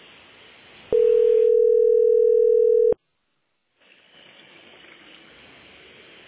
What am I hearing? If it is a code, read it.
Ringback tone (North America), 440 + 480 Hz.